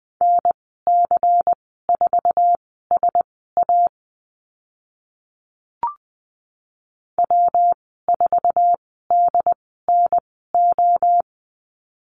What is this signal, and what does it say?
Morse code, heard as a keyed tone, about 20 wpm, spaced as for 15 wpm, 700 Hz. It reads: NC4SA W4DNO
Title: 128 Lesson 21g - Number 6 - Call Signs 20wpm